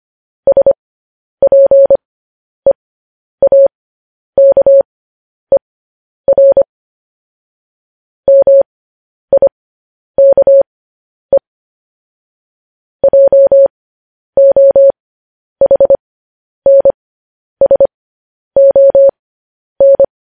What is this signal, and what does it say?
Morse code, heard as a keyed tone, about 25 wpm, spaced as for 10 wpm, 550 Hz. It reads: SPEAKER MIKE JOHNSON